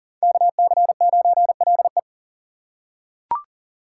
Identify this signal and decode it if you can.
Morse code, heard as a keyed tone, about 40 wpm, 700 Hz. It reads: KC9LE
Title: USA Call Signs - 5 Characters - 4x Character Spacing - 40wpm